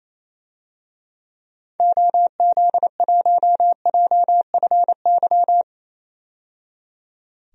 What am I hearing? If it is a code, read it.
Morse code, heard as a keyed tone, about 28 wpm, 700 Hz. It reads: OZ1JFY
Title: DXCC-100 Call Signs - 6 Characters - 28wpm